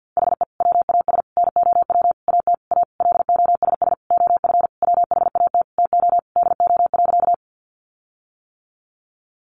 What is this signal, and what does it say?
Morse code, heard as a keyed tone, about 50 wpm, 700 Hz. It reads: HE WAS NOW AT A LOSS OF WHAT TO DO.
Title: Sentences from Top 1400 Words 50wpm